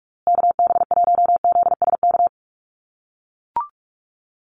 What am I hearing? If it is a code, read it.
Morse code, heard as a keyed tone, about 44 wpm, 690 Hz. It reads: KB1ZSK
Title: USA Call Signs Repeated 2x 45wpm